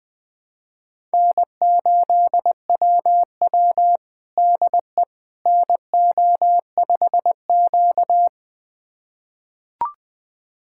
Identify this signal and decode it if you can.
Morse code, heard as a keyed tone, about 20 wpm, 700 Hz. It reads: N8WW DE NO5Q
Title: USA Callsign DE Callsign - 4 Characters - 20wpm